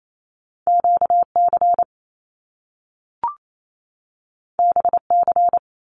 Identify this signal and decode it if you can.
Morse code, heard as a keyed tone, about 28 wpm, 700 Hz. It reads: QC BC
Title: Canadian Provinces and Territories 28wpm